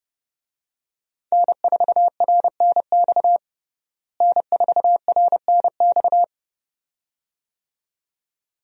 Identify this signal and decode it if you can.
Morse code, heard as a keyed tone, about 30 wpm, 700 Hz. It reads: N4RNX N4RNX
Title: USA Call Signs - 5 Characters - Repeated 2x - 30wpm